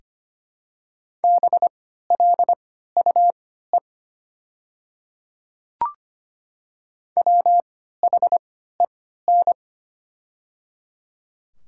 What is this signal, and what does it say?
Morse code, heard as a keyed tone, about 25 wpm, spaced as for 14 wpm, 690 Hz. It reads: BLUE WHEN
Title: 3x Character Spacing - Top 500 Words - 25wpm